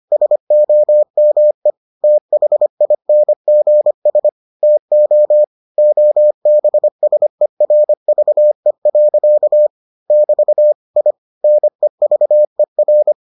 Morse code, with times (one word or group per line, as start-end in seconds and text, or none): 0.1-1.7 SOME
2.0-4.3 THINGS
4.6-5.4 TO
5.8-9.7 OBSERVE.
10.1-10.7 <BT>
11.0-11.1 I
11.4-13.1 NEVER